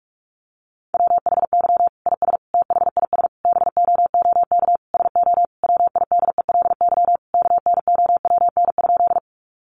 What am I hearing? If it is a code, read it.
Morse code, heard as a keyed tone, about 44 wpm, 700 Hz. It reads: WHY IS THIS BOOK SO WIDELY KNOWN?
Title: Sentences from Top 1200 Words 45wpm